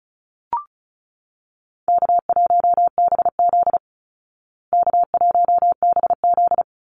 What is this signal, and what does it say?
Morse code, heard as a keyed tone, about 34 wpm, 700 Hz. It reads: K1BZ K1BZ
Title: Call Signs Repeated 3x - 35wpm